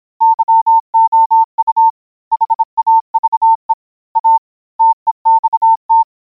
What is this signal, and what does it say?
Morse code, heard as a keyed tone, about 26 wpm, 900 Hz. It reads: YOU HAVE A TEXT